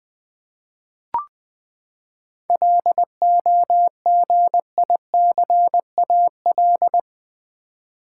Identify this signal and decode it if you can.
Morse code, heard as a keyed tone, about 20 wpm, 700 Hz. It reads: LOGICAL
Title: Seven Letter Words - 20wpm